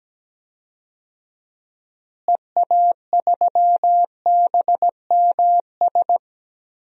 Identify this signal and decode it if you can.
Morse code, heard as a keyed tone, about 17 wpm, 700 Hz. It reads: EA3BMS
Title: DXCC-100 Call Signs - 6 Characters - Repeated 2x - 17wpm